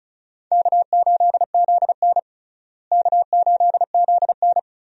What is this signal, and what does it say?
Morse code, heard as a keyed tone, about 34 wpm, 700 Hz. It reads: K8ZN K8ZN
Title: USA Call Signs - 4 Characters - Repeated 3x - 35wpm